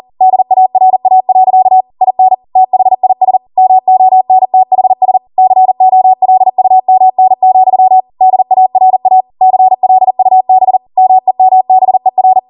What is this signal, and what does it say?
Morse code, heard as a keyed tone, about 40 wpm, 750 Hz. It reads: DARA. IN THIS MONTHS COLUMN, DARA CLUB MEMBER